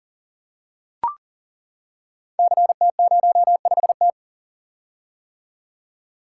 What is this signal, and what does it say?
Morse code, heard as a keyed tone, about 41 wpm, 700 Hz. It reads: CT05T
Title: Code Groups - 5 Letter:Numbers - 40wpm